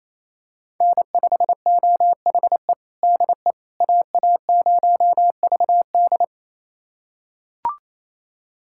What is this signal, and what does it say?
Morse code, heard as a keyed tone, about 28 wpm, 700 Hz. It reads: N5OHE DE AA0VD
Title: USA Callsign DE Callsign - 5 Characters - 28wpm